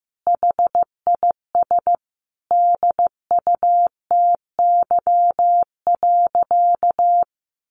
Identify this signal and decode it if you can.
Morse code, heard as a keyed tone, about 15 wpm, 700 Hz. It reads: HIS DUTY.
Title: Sentences from Top 1500 Words 15wpm